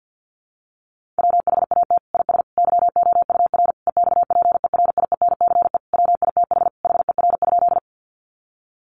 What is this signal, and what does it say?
Morse code, heard as a keyed tone, about 49 wpm, 700 Hz. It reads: WHAT IS YOUR EXPERIENCE WITH HER?
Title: Sentences from Top 600 Words 50wpm